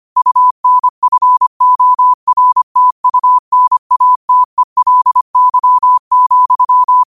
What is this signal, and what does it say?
Morse code, heard as a keyed tone, about 25 wpm, 1000 Hz. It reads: UNFORTUNATELY,